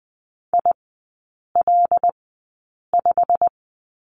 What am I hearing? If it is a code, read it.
Morse code, heard as a keyed tone, about 20 wpm, spaced as for 8 wpm, 700 Hz. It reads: IL5
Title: 127 Lesson 21e - Number 6 - Three Characters 20wpm